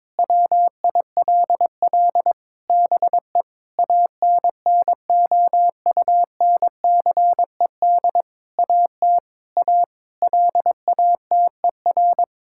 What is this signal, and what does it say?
Morse code, heard as a keyed tone, about 22 wpm, 700 Hz. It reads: WILL BE ANNOUNCED AT A LATER
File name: Sentences with 7-Word Phrases - 22wpm